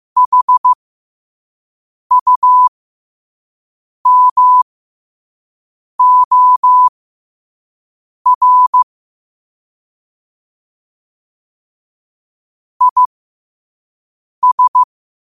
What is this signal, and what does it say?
Morse code, heard as a keyed tone, about 15 wpm, spaced as for 5 wpm, 1000 Hz. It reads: HUMOR IS
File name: qotdfiles / 5wpm